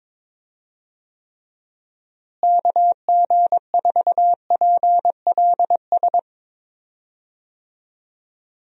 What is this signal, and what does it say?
Morse code, heard as a keyed tone, about 22 wpm, 700 Hz. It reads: KG4PLS
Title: USA Call Signs - 6 Characters - Repeated 3x - 22wpm